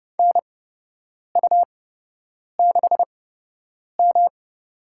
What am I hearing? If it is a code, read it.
Morse code, heard as a keyed tone, about 31 wpm, spaced as for 8 wpm, 700 Hz. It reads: NU6M
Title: USA Call Signs - 4 Characters - 8x Character Spacing - 30wpm